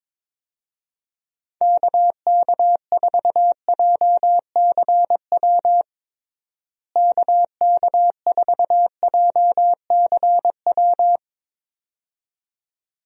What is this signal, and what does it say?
Morse code, heard as a keyed tone, about 22 wpm, 700 Hz. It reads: KK4JCW KK4JCW
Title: USA Call Signs - 6 Characters - Repeated 2x - 22wpm